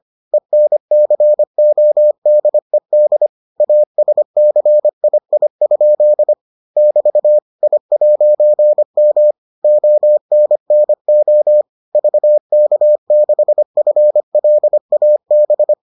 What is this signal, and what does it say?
Morse code, heard as a keyed tone, about 25 wpm, 600 Hz. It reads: ENCODED ASCII? <BT> I'M ONNO VK6FLAB